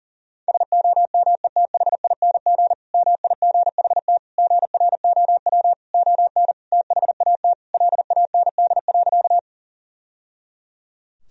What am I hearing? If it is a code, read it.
Morse code, heard as a keyed tone, about 40 wpm, 700 Hz. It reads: SOMETHING MIGHT GROW ON THAT LAND.
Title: Sentences from Top 1000 Words - Review Part 1 - 40wpm